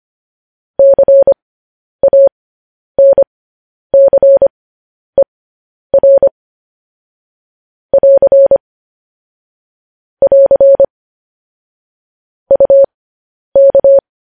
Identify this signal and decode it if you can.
Morse code, heard as a keyed tone, about 25 wpm, spaced as for 10 wpm, 550 Hz. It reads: CANCER <AR> <AR> UK
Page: news headlines in morse code at 10 wpm